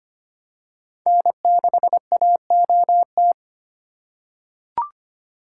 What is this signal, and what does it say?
Morse code, heard as a keyed tone, about 25 wpm, 700 Hz. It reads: N6AOT
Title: USA Call Signs - 2x Character Spacing - 25wpm